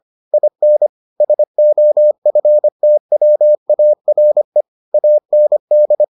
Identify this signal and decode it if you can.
Morse code, heard as a keyed tone, about 25 wpm, 600 Hz. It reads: IN SOFTWARE AND